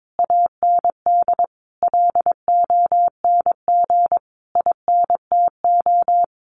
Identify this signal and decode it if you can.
Morse code, heard as a keyed tone, about 22 wpm, 700 Hz. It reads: AND LONG INTO
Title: Sentences from Top 1500 Words 22wpm